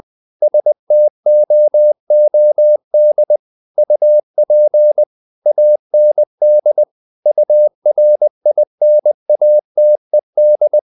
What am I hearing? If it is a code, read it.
Morse code, heard as a keyed tone, about 20 wpm, 600 Hz. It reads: STOOD UP AND URINATED
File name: the old man and the sea / the old man and the sea part 1 20wpm